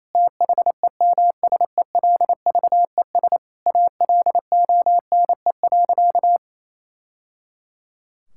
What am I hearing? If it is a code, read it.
Morse code, heard as a keyed tone, about 28 wpm, 700 Hz. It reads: THEMSELVES ALONE.
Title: Sentences from Top 1000 Words - Review Part 5 - 28wpm